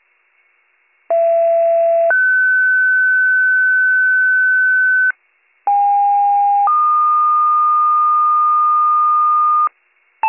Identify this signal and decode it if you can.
Two-tone page, heard as a one-second tone, then a three-second tone, about 670 then 1500 Hz, about 810 then 1220 Hz, with noise. It is heard from 1.1 to 9.7 s.